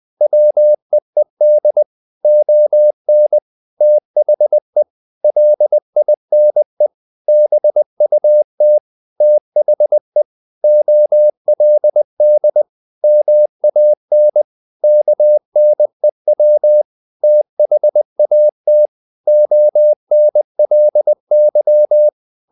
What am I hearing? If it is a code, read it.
Morse code, heard as a keyed tone, about 20 wpm, 600 Hz. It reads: WEED ON THE LINE BUT THE OLD MAN KNEW THAT ONLY
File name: the old man and the sea / the old man and the sea part 1 20wpm